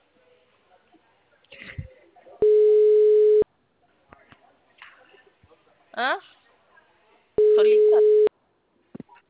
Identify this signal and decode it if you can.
Ringback tone (Europe, 425 Hz), with speech.
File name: out- -2005- - - .1264